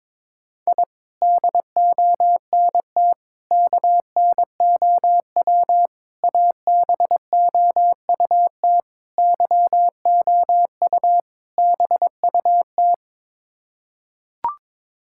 Morse code, heard as a keyed tone, about 22 wpm, 700 Hz. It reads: I DONT KNOW ABOUT YOU BUT
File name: practice / 22wpm / Sets of 6 Words - 22wpm